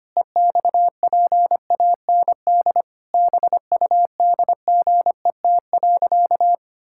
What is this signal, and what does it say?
Morse code, heard as a keyed tone, about 25 wpm, 700 Hz. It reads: EXPAND BUDGET.
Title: Pangram Challenge v2 - 25wpm